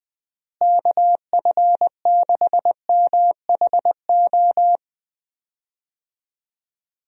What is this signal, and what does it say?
Morse code, heard as a keyed tone, about 21 wpm, 700 Hz. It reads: KF6MHO
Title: USA Call Signs - 6 Characters - Repeated 2x - 20wpm